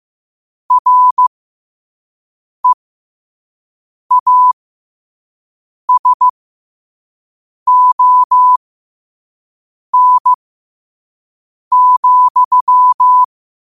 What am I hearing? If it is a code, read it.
Morse code, heard as a keyed tone, about 15 wpm, spaced as for 5 wpm, 1000 Hz. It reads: REASON,
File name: qotdfiles / 5wpm